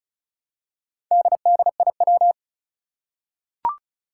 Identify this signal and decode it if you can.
Morse code, heard as a keyed tone, about 36 wpm, 700 Hz. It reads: DDIW
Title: Four Challenging Letter-Number Combinations 35wpm